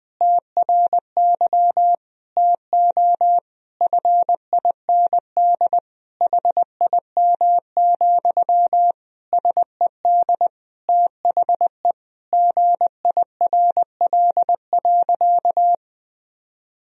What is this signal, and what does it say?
Morse code, heard as a keyed tone, about 20 wpm, 700 Hz. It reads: TRY TO FIND HIM, SED THE GIRL.
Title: Sentences from Top 400 Words and Abbreviations 20wpm